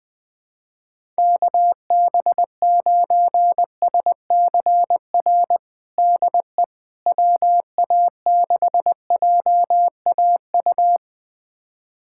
Morse code, heard as a keyed tone, about 21 wpm, 700 Hz. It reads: KB9SCR DE WA6JAU
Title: USA Callsign DE Callsign - 6 Characters - 20wpm